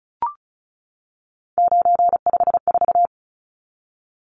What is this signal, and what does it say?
Morse code, heard as a keyed tone, about 34 wpm, 700 Hz. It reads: 954
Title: Numbers - 3-Digit 35wpm